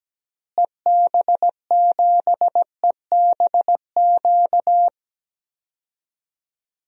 Morse code, heard as a keyed tone, about 17 wpm, 700 Hz. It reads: EB7EBQ
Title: DXCC-100 Call Signs - 6 Characters - Repeated 3x - 17wpm